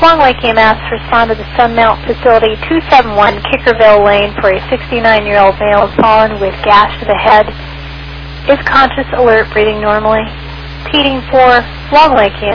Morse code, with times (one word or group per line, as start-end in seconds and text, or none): none